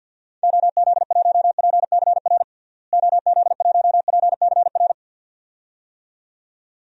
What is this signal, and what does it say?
Morse code, heard as a keyed tone, about 52 wpm, 700 Hz. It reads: OZ1PCR OZ1PCR
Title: DXCC-100 Call Signs - 6 Characters - Repeated 2x - 50wpm